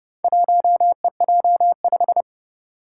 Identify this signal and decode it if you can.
Morse code, heard as a keyed tone, about 30 wpm, 700 Hz. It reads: 1EJ5
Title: Four Challenging Letter-Number Combinations 30wpm